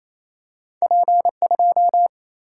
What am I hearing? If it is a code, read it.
Morse code, heard as a keyed tone, about 28 wpm, 700 Hz. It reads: P 2
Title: Single-Letter Single-Number 28wpm